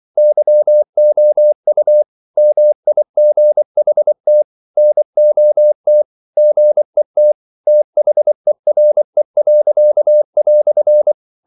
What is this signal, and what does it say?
Morse code, heard as a keyed tone, about 24 wpm, 600 Hz. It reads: YOU MIGHT NOT GET THERE."
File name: CW / Practice / EnglishQuotations0324